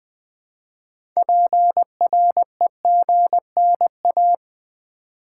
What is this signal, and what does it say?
Morse code, heard as a keyed tone, about 20 wpm, 700 Hz. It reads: PREGNA
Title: Build A Word - 3 Letters At A Time - 8-Letter Words - 20wpm